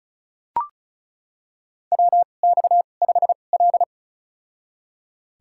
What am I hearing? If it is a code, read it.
Morse code, heard as a keyed tone, about 36 wpm, spaced as for 25 wpm, 700 Hz. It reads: WX5L
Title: USA Call Signs - 4 Characters - 2x Character Spacing - 35wpm